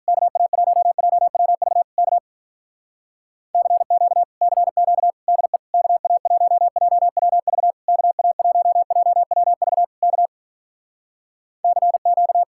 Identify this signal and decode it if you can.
Morse code, heard as a keyed tone, about 48 wpm, 700 Hz. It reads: KA1JWV K CQ CQ DE KA1JWV KA1JWV K CQ